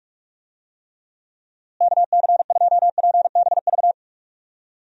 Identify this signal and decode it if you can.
Morse code, heard as a keyed tone, about 46 wpm, 700 Hz. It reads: KC2PBV
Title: USA Call Signs - 6 Characters - Repeated 3x - 45wpm